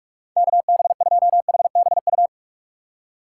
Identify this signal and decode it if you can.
Morse code, heard as a keyed tone, about 46 wpm, 700 Hz. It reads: KB2HBU
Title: USA Call Signs - 6x Character Spacing - 45wpm